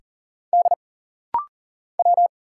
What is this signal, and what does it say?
Morse code, heard as a keyed tone, about 40 wpm, 700 Hz. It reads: D W
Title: Single Letter-Number Alt - Rapid-Fire 40wpm